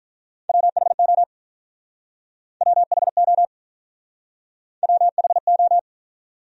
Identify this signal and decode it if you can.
Morse code, heard as a keyed tone, about 45 wpm, 700 Hz. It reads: WHO WHO WHO